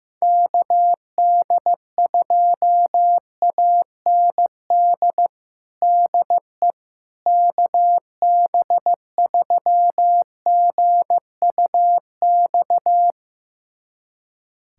Morse code, heard as a keyed tone, about 15 wpm, 700 Hz. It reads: KD2AND DE KB3GUX
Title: USA Callsign DE Callsign - 6 Characters - 15wpm